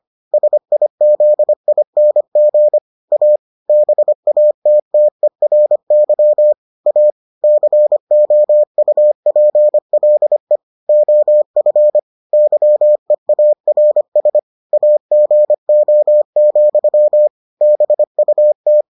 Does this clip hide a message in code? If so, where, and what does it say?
Yes: Morse code from 0.3 to 18.8 s, heard as a keyed tone, about 25 wpm, 600 Hz. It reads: SIZING A BATTERY A COUPLE OF YEARS AGO, BUT